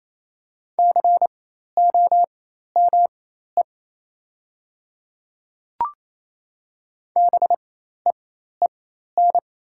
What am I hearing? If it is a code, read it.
Morse code, heard as a keyed tone, about 28 wpm, spaced as for 13 wpm, 700 Hz. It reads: COME BEEN
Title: 4x Character Spacing - Top 100 Words - 28wpm